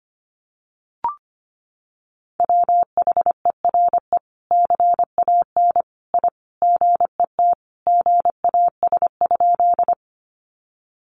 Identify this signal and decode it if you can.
Morse code, heard as a keyed tone, about 25 wpm, 700 Hz. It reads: WHERE CAN I GET GAS?
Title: Sentences from Top 1100 Words 25wpm